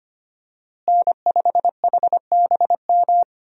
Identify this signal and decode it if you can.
Morse code, heard as a keyed tone, about 25 wpm, 700 Hz. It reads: N5HBM
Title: USA Call Signs - 5 Characters - Repeated 3x - 25wpm